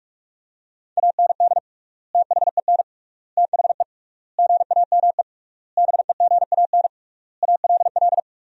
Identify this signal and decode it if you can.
Morse code, heard as a keyed tone, about 46 wpm, 700 Hz. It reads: AND THEN THE GAME BEGAN ALL
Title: 3x Word-Spacing - Sentences from Top 500 Words - 45wpm